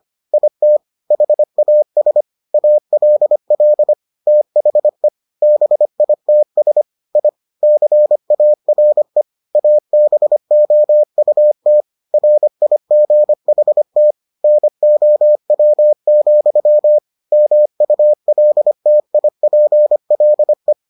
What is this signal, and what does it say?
Morse code, heard as a keyed tone, about 25 wpm, 600 Hz. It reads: IT HAS ALL THE BITS I CARE ABOUT RIGHT NOW, MULTIPLE